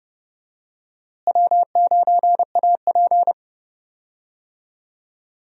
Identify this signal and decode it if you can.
Morse code, heard as a keyed tone, about 31 wpm, 700 Hz. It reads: W9AP
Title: USA Call Signs - 4 Characters - Repeated 3x - 30wpm